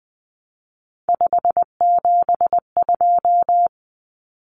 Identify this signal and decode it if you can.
Morse code, heard as a keyed tone, about 20 wpm, 700 Hz. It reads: 572
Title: Common Cut Numbers - 3-Digit 20wpm